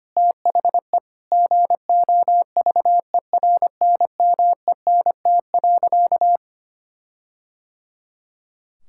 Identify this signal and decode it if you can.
Morse code, heard as a keyed tone, about 25 wpm, 700 Hz. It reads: THE GOVERNMENT.
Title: Sentences from Top 1600 Words 25wpm